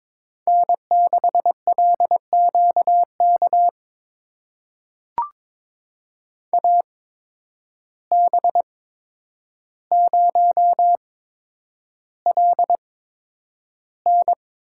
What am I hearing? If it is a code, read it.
Morse code, heard as a keyed tone, about 22 wpm, spaced as for 9 wpm, 700 Hz. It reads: N6LQK AB0LN
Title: USA Call Signs - 5 Characters - 8x Character Spacing - 22wpm